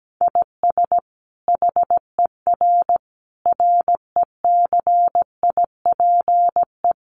Morse code, heard as a keyed tone, about 17 wpm, 700 Hz. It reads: IS HER RECIPE
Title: Sentences from Top 1200 Words 17wpm